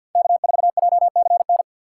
Morse code, heard as a keyed tone, about 52 wpm, 700 Hz. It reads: K4JCN
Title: USA Call Signs - 5 Characters - 3x Character Spacing - 50wpm